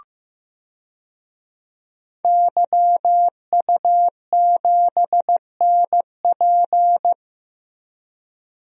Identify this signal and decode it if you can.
Morse code, heard as a keyed tone, about 15 wpm, 700 Hz. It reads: YU7NP
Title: DXCC-100 Call Signs - 5 Characters - Repeated 2x - 15wpm